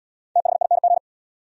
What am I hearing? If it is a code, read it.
Morse code, heard as a keyed tone, about 73 wpm, 700 Hz. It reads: THING